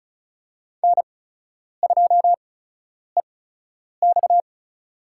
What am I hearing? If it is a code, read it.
Morse code, heard as a keyed tone, about 36 wpm, spaced as for 10 wpm, 700 Hz. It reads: N2EX
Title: USA Call Signs - 8x Character Spacing - 35wpm